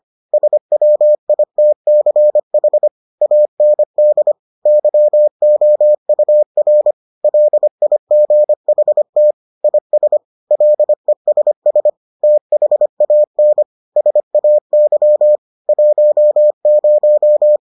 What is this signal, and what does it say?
Morse code, heard as a keyed tone, about 25 wpm, 600 Hz. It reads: SWITCH AND YOUR LIGHT IS LESS THAN SAY 10